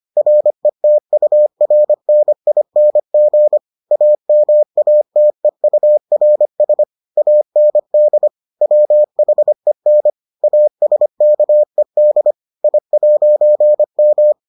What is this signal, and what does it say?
Morse code, heard as a keyed tone, about 25 wpm, 600 Hz. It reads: RETURNING AMATEURS AND WHEN ASKED I'M